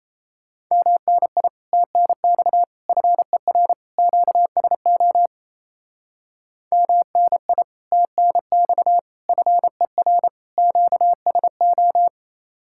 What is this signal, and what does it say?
Morse code, heard as a keyed tone, about 36 wpm, 700 Hz. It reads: MNI TNX FER QSO MNI TNX FER QSO